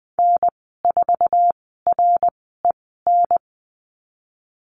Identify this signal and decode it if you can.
Morse code, heard as a keyed tone, about 19 wpm, spaced as for 14 wpm, 700 Hz. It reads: N4REN
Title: 076 Lesson 13g - Letter L - Call Signs 20wpm